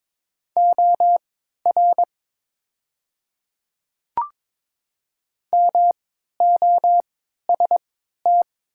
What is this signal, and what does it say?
Morse code, heard as a keyed tone, about 22 wpm, spaced as for 12 wpm, 700 Hz. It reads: OR MOST
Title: 3x Character Spacing - Top 100 Words - 22wpm